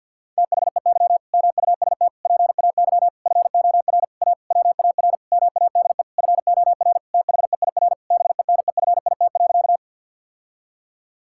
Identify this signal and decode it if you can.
Morse code, heard as a keyed tone, about 50 wpm, 700 Hz. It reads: THEY MUST PAY FOR A WAR MADE FOR THEIR BENEFIT.
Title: Sentences from Top 1000 Words - Review Part 6 - 50wpm